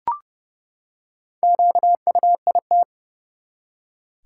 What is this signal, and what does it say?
Morse code, heard as a keyed tone, about 30 wpm, 700 Hz. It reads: QUIT
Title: Four Letter Words - 30wpm